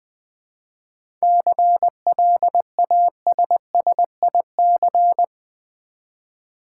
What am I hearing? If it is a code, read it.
Morse code, heard as a keyed tone, about 20 wpm, 700 Hz. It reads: CLASSIC
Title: Seven Letter Words - 20wpm